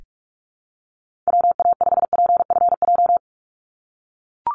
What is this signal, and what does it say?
Morse code, heard as a keyed tone, about 44 wpm, 700 Hz. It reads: WA5PFJ
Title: Call Signs - 45wpm